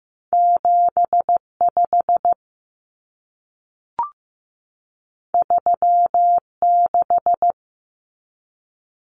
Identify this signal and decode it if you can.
Morse code, heard as a keyed tone, about 15 wpm, 700 Hz. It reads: 75 36
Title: Common Cut Numbers - 2-Digit 15wpm